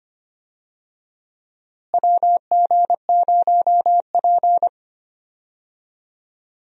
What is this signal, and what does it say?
Morse code, heard as a keyed tone, about 25 wpm, 700 Hz. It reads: WG0P
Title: USA Call Signs - 4 Characters - Repeated 3x - 25wpm